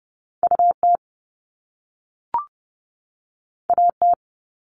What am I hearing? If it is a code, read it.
Morse code, heard as a keyed tone, about 30 wpm, 700 Hz. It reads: UT AT
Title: Top Two Letter Combinations in Words 30wpm